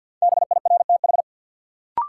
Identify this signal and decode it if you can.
Morse code, heard as a keyed tone, about 51 wpm, 700 Hz. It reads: BIRTH
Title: Sentences from Top 800 Words - New Word Drill - 50wpm